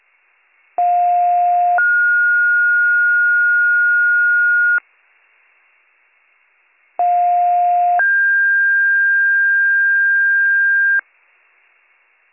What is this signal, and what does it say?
Two-tone page, heard as a one-second tone, then a three-second tone, about 710 then 1430 Hz, about 710 then 1600 Hz. It is heard from 0.8 to 11.0 s.